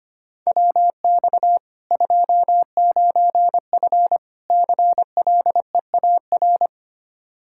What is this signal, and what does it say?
Morse code, heard as a keyed tone, about 25 wpm, 700 Hz. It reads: WX 29F CLEAR
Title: Practice QSO - Exchange 2 - Info - 25wpm